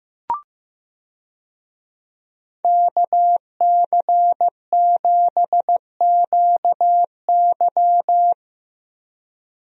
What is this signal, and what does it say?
Morse code, heard as a keyed tone, about 15 wpm, 700 Hz. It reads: KC7QY
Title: USA Call Signs - 5 Characters - Repeated 2x - 15wpm